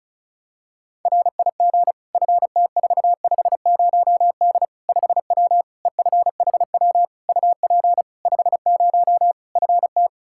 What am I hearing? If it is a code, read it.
Morse code, heard as a keyed tone, about 34 wpm, 700 Hz. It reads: RIG FT450D 5W EFHW UP 50 FT
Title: Practice QSO - Exchange 2 - Info - 35wpm